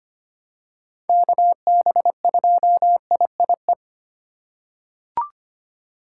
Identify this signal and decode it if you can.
Morse code, heard as a keyed tone, about 25 wpm, 700 Hz. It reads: KB2IIE
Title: USA Call Signs - 6 Characters - 2x Character Spacing - 25wpm